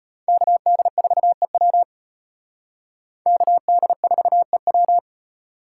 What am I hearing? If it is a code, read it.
Morse code, heard as a keyed tone, about 32 wpm, 700 Hz. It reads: KD4EW KD4EW